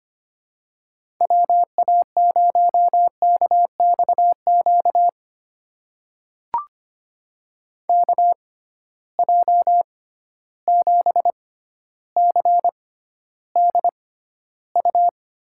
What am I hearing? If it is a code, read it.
Morse code, heard as a keyed tone, about 25 wpm, spaced as for 13 wpm, 700 Hz. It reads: WA0KXQ KJ7CDU
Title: USA Call Signs - 6 Characters - 6x Character Spacing - 25wpm